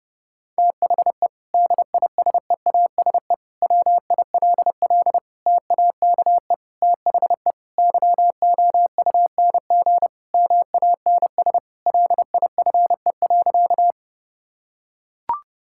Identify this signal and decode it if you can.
Morse code, heard as a keyed tone, about 30 wpm, 700 Hz. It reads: THE DISEASE WILL TAKE THE YOUNG MANS LIFE.
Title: Sentences from Top 1000 Words - Review Part 6 - 30wpm